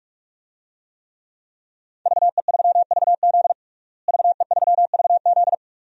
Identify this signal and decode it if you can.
Morse code, heard as a keyed tone, about 41 wpm, 700 Hz. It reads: VE3VZ VE3VZ
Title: DXCC-100 Call Signs - 5 Characters - Repeated 2x - 45wpm